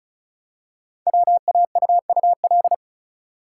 Morse code, heard as a keyed tone, about 36 wpm, 690 Hz. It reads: WAUUL
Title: Code Groups - 5 Letters - 35wpm